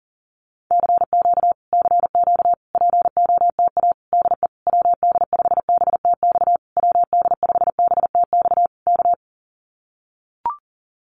Morse code, heard as a keyed tone, about 40 wpm, 700 Hz. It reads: CQ CQ POTA DE WD5BTX WD5BTX K
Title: Common QSO Phrases Encore 40wpm